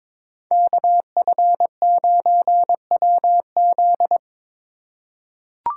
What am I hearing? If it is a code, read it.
Morse code, heard as a keyed tone, about 21 wpm, 700 Hz. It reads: KF9WZ
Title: USA Call Signs - 3x Character Spacing - 22wpm